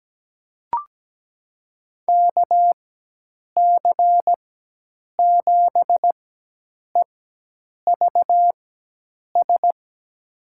Morse code, heard as a keyed tone, about 17 wpm, spaced as for 8 wpm, 700 Hz. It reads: KC7EVS
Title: USA Call Signs - 6 Characters - 4x Character Spacing - 17wpm